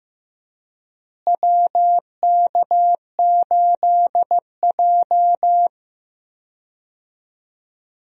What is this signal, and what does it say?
Morse code, heard as a keyed tone, about 15 wpm, 700 Hz. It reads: WK8J
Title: USA Call Signs - 4 Characters - Repeated 2x - 15wpm